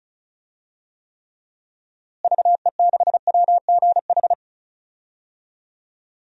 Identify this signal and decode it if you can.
Morse code, heard as a keyed tone, about 34 wpm, 700 Hz. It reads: VE6WGH
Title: DXCC-100 Call Signs - 6 Characters - Repeated 2x - 35wpm